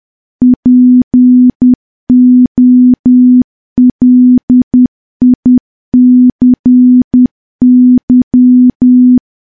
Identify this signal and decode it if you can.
Morse code, heard as a keyed tone, about 10 wpm, 250 Hz. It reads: POLICY
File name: morse / morse-training-10WPM